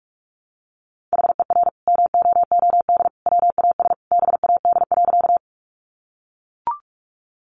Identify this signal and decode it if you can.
Morse code, heard as a keyed tone, about 44 wpm, 700 Hz. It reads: HER MOOD WAS BAD.
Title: Sentences from Top 1100 Words 45wpm